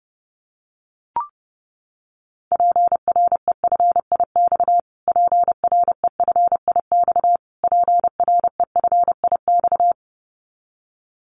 Morse code, heard as a keyed tone, about 30 wpm, 700 Hz. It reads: PREFIX PREFIX PREFIX
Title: Callsign Prefixes for Top 25 DXCC - Repeated 3x - 30wpm